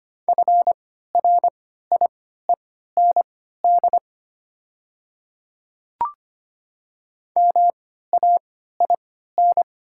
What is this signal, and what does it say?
Morse code, heard as a keyed tone, about 25 wpm, spaced as for 14 wpm, 700 Hz. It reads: FRIEND MAIN
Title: 3x Character Spacing - Top 300 Words - 25wpm